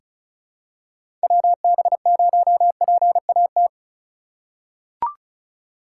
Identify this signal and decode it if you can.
Morse code, heard as a keyed tone, about 36 wpm, 700 Hz. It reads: WB0PAT
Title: USA Call Signs - 6 Characters - 3x Character Spacing - 35wpm